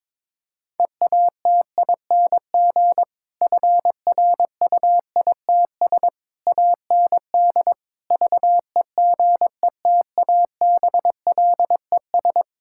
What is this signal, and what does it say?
Morse code, heard as a keyed tone, about 22 wpm, 700 Hz. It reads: EATING FRUITS AND VEGETABLES
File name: Difficult Sentences with Hints - 22wpm